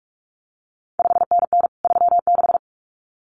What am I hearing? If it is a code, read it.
Morse code, heard as a keyed tone, about 46 wpm, 700 Hz. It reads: 5NN 36
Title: CQ WW DX Contest Exchange Encore - 45wpm